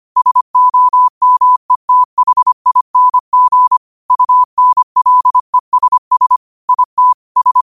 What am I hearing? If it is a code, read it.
Morse code, heard as a keyed tone, about 25 wpm, 1000 Hz. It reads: SOMETHING UNLESS IT S
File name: qotdfiles / 25wpm